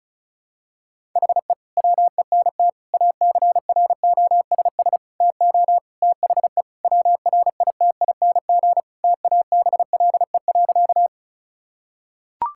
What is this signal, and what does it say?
Morse code, heard as a keyed tone, about 36 wpm, 700 Hz. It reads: HE WENT ACROSS TO THE WRITING TABLE.
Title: Sentences from Top 1000 Words - Review Part 5 - 35wpm